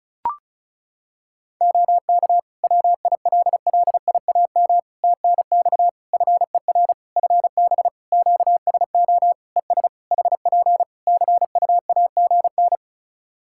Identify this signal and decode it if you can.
Morse code, heard as a keyed tone, about 36 wpm, 700 Hz. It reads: OK WILLIAM TNX FER FB QSO ES HP CUAGN
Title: Practice QSO - Exchange 4 - Ending - 35wpm